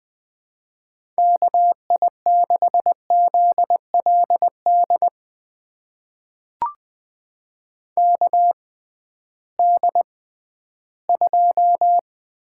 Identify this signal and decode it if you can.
Morse code, heard as a keyed tone, about 20 wpm, spaced as for 13 wpm, 700 Hz. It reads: KI6ZLD KD2
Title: USA Call Signs - 6x Character Spacing - 20wpm